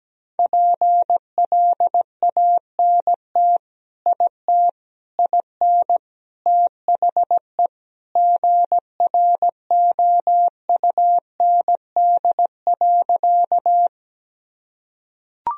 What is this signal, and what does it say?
Morse code, heard as a keyed tone, about 17 wpm, 700 Hz. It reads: PLANT IT IN THE GROUND.
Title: Sentences from Top 1000 Words - Review Part 1 - 17wpm